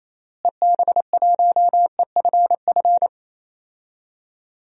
Morse code, heard as a keyed tone, about 27 wpm, 700 Hz. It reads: EB1EFF
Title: DXCC-100 Call Signs - 6 Characters - Repeated 3x - 28wpm